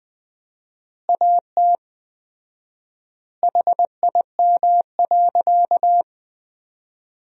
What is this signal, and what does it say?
Morse code, heard as a keyed tone, about 21 wpm, 700 Hz. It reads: AT HIM.
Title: 4x Word-Spacing - Sentences from Top 100 Words - 20wpm